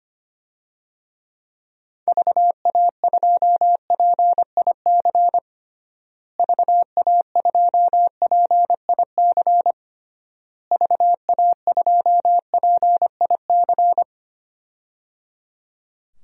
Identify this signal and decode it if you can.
Morse code, heard as a keyed tone, about 25 wpm, 700 Hz. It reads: VA2PIC VA2PIC VA2PIC
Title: DXCC-100 Call Signs - 6 Characters - Repeated 3x - 25wpm